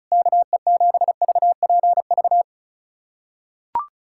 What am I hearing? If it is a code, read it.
Morse code, heard as a keyed tone, about 36 wpm, 700 Hz. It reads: KE7VPV
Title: USA Callsign DE Callsign - 6 Characters - 35wpm